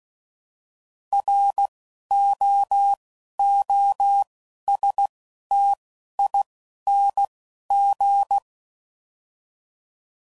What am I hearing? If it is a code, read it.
Morse code, heard as a keyed tone, about 16 wpm, 800 Hz. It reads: R O O S T I N G
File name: Prefixes / By First Letter / Rs